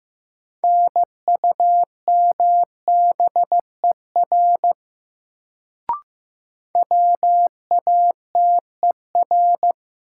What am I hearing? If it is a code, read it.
Morse code, heard as a keyed tone, about 15 wpm, 700 Hz. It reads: NUMBER WATER
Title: Sentences from Top 100 Words - New Word Drill - 15wpm